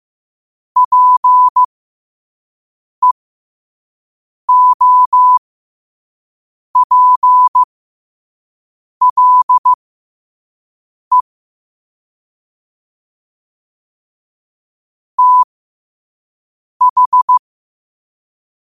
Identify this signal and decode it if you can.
Morse code, heard as a keyed tone, about 15 wpm, spaced as for 5 wpm, 1000 Hz. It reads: PEOPLE TH
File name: qotdfiles / 5wpm